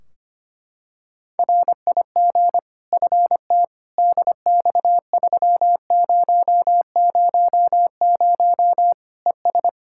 Morse code, heard as a keyed tone, about 25 wpm, 700 Hz. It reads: RIG FT DX3000 ES
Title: K1USN SST QSO Practice 25wpm